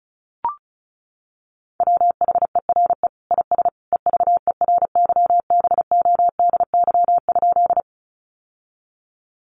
Morse code, heard as a keyed tone, about 34 wpm, 700 Hz. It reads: WHERE IS EVERYBODY?
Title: Sentences from Top 1400 Words 35wpm